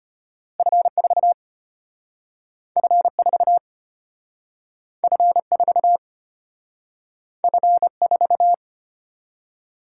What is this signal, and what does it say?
Morse code, heard as a keyed tone, about 34 wpm, 700 Hz. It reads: F4 F4 F4 F4